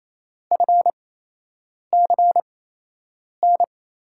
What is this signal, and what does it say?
Morse code, heard as a keyed tone, about 27 wpm, spaced as for 8 wpm, 700 Hz. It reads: FCN
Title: USA Call Signs - 6 Characters - 8x Character Spacing - 28wpm